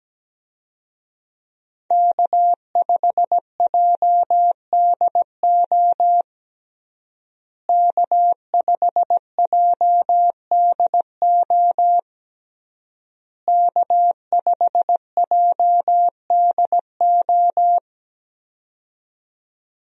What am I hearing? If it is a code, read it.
Morse code, heard as a keyed tone, about 17 wpm, 700 Hz. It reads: K5JDO K5JDO K5JDO
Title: USA Call Signs - 5 Characters - Repeated 3x - 17wpm